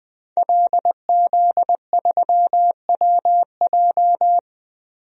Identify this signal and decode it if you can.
Morse code, heard as a keyed tone, about 20 wpm, 700 Hz. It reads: LZ3WJ
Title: DXCC-100 Call Signs - 5 Characters - Repeated 2x - 20wpm